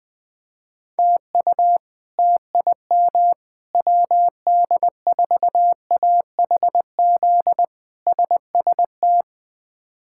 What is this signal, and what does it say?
Morse code, heard as a keyed tone, about 20 wpm, 700 Hz. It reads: TU TIM WD4AHZ SST
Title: K1USN SST QSO Practice - 20wpm